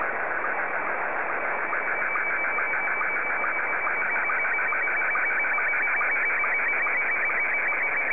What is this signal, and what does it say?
SSTV (slow-scan television), Scottie 1.
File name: sstv / S1-VK7ZCF